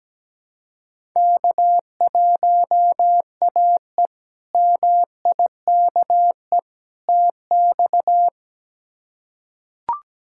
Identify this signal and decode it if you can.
Morse code, heard as a keyed tone, about 17 wpm, 700 Hz. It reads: K1AE MIKE TX
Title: K1USN SST QSO Practice - 17wpm